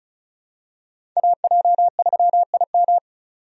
Morse code, heard as a keyed tone, about 36 wpm, 700 Hz. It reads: AJ3IM
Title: USA Call Signs - 5 Characters - 35wpm